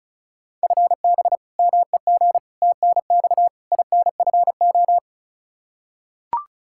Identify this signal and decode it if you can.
Morse code, heard as a keyed tone, about 36 wpm, 700 Hz. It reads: FB MEG TNX INFO
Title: Practice QSO - Exchange 3 - More Info - 35wpm